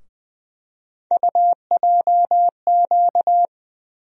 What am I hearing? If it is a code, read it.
Morse code, heard as a keyed tone, about 20 wpm, 700 Hz. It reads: UJQ
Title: USA Call Sign Postfixes - 3 Letters 20wpm